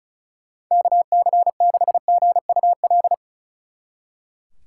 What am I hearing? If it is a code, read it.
Morse code, heard as a keyed tone, about 46 wpm, 700 Hz. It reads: KC6GUL